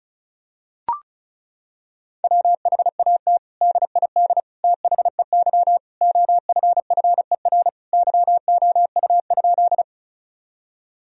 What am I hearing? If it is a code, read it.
Morse code, heard as a keyed tone, about 36 wpm, 700 Hz. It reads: WHAT DID THEY OFFER YOU?
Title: Sentences from Top 1000 Words - Review Part 4 - 35wpm